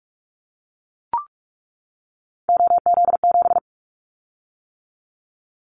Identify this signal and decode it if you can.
Morse code, heard as a keyed tone, about 44 wpm, 700 Hz. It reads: OZ7
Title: Callsign Prefixes for Top 100 DXCC - 45wpm